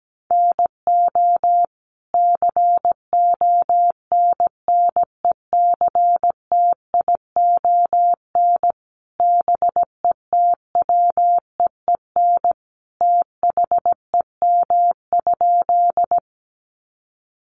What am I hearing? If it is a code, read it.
Morse code, heard as a keyed tone, about 17 wpm, 700 Hz. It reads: NO CONNECTION BETWEEN THEM?
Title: Sentences from Top 1300 Words 17wpm